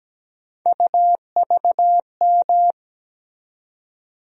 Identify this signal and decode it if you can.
Morse code, heard as a keyed tone, about 17 wpm, 700 Hz. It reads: UVM
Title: USA Call Sign Postfixes - 3 Letters 17wpm